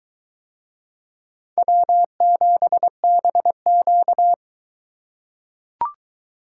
Morse code, heard as a keyed tone, about 21 wpm, 700 Hz. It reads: W7BQ